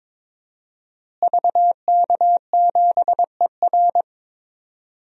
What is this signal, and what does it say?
Morse code, heard as a keyed tone, about 22 wpm, 700 Hz. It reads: VK7ER
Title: DXCC-100 Call Signs - 5 Characters - Repeated 2x - 22wpm